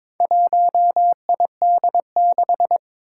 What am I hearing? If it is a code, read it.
Morse code, heard as a keyed tone, about 22 wpm, 700 Hz. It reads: 1ID6
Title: Four Challenging Letter-Number Combinations 22wpm